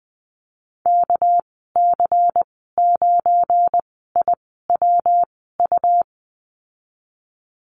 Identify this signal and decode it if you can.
Morse code, heard as a keyed tone, about 19 wpm, spaced as for 14 wpm, 700 Hz. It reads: KC9IWU
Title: 213 Lesson 34g - Number 8 - Call Signs 20wpm